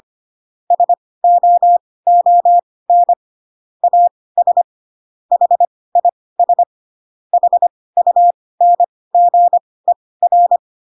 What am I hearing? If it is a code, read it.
Morse code, heard as a keyed tone, about 25 wpm, spaced as for 18 wpm, 700 Hz. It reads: SOON AS HIS HUNGER